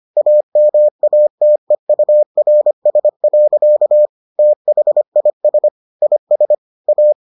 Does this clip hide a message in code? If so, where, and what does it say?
Yes: Morse code from 0.2 to 7.1 s, heard as a keyed tone, about 25 wpm, 600 Hz. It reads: AMATEURS. THIS IS A